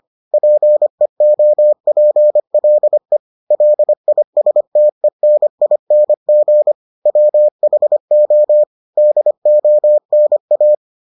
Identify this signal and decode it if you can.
Morse code, heard as a keyed tone, about 25 wpm, 600 Hz. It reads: PEOPLE LISTENING WHO DONA